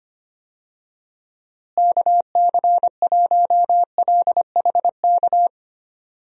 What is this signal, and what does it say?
Morse code, heard as a keyed tone, about 25 wpm, 700 Hz. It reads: KC1LHK
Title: USA Call Signs - 6 Characters - Repeated 3x - 25wpm